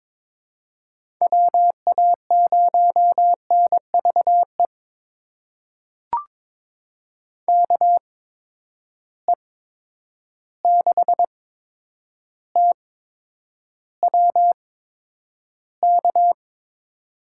Morse code, heard as a keyed tone, about 21 wpm, spaced as for 9 wpm, 700 Hz. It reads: WA0NVE KE6TWK
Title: USA Call Signs - 6 Characters - 8x Character Spacing - 22wpm